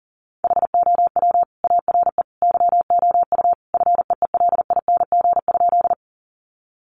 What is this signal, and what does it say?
Morse code, heard as a keyed tone, about 40 wpm, 700 Hz. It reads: HOW ARE YOU FEELING?
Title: Sentences from Top 1200 Words 40wpm